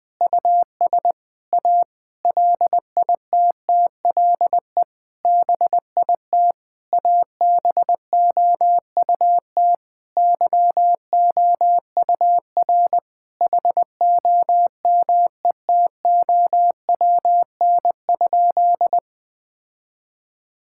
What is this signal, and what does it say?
Morse code, heard as a keyed tone, about 20 wpm, 700 Hz. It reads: US A LITTLE BIT ABOUT YOUR HOMETOWN?
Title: Sentences with 8-Word Phrases - Spoken Beginning - 20wpm